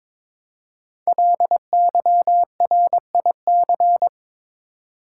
Morse code, heard as a keyed tone, about 22 wpm, 700 Hz. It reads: LYRIC
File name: 5 Letter Words - First 3 Letters Given - 22wpm